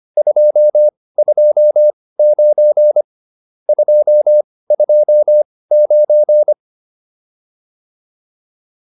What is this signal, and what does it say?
Morse code, heard as a keyed tone, about 25 wpm, spaced as for 18 wpm, 600 Hz. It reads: 229 229